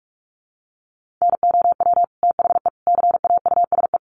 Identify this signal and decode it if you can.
Morse code, heard as a keyed tone, about 44 wpm, 700 Hz. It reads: NOW THE CAUSE
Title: Sentences from Top 1500 Words 45wpm